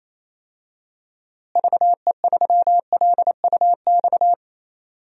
Morse code, heard as a keyed tone, about 28 wpm, 700 Hz. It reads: VE3LUX
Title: DXCC-100 Call Signs - 6 Characters - Repeated 3x - 28wpm